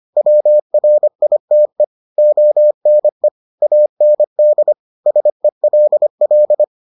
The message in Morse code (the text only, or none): WRITE ONE AND SELL